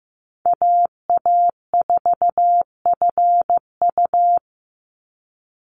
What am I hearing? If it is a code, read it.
Morse code, heard as a keyed tone, about 15 wpm, 700 Hz. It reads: AA4FU
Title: Call Signs Repeated 2x - 15wpm